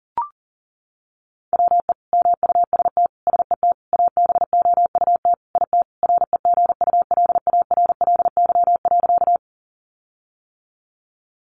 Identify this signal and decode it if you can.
Morse code, heard as a keyed tone, about 40 wpm, 700 Hz. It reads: WE MUST SET ABOUT IT REGULARLY.
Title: Sentences from Top 1400 Words 40wpm